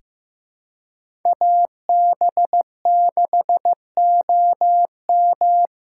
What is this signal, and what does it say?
Morse code, heard as a keyed tone, about 15 wpm, 700 Hz. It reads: AB6OM
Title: USA Call Signs - 3x Character Spacing - 15wpm